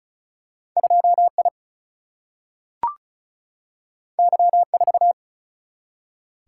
Morse code, heard as a keyed tone, about 36 wpm, 700 Hz. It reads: 2 I Y 4
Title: Single-Letter Single-Number 35wpm